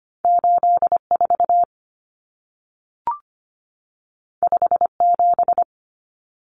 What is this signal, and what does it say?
Morse code, heard as a keyed tone, about 25 wpm, 700 Hz. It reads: 84 57
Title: Common Cut Numbers - 2-Digit 25wpm